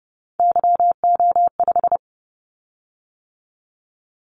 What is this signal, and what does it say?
Morse code, heard as a keyed tone, about 30 wpm, 700 Hz. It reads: YO5
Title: Callsign Prefixes for Top 25 DXCC - 30wpm